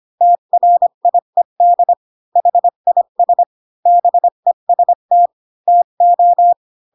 Morse code, heard as a keyed tone, about 25 wpm, 700 Hz. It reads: TRIED HIS BEST TO
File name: Pinocchio / Pinocchio ch3 20 25 ik1hge